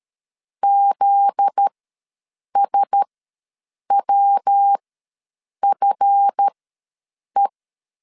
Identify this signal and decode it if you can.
Morse code, heard as a keyed tone, about 13 wpm, spaced as for 7 wpm, 800 Hz. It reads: ZSWFE
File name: LEZIONI / 7 GF9Z / 7 GF9Z ws 12wpm 3321